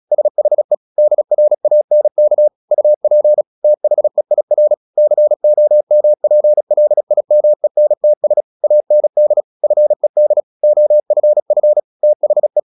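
Morse code, heard as a keyed tone, about 36 wpm, 600 Hz. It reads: SHE DRANK UP THEIR COMPLIMENTS AND FED OFF THE
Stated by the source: Morse practice at 36 wpm, text of Ken Follett's novel Never, from CW med Gnister.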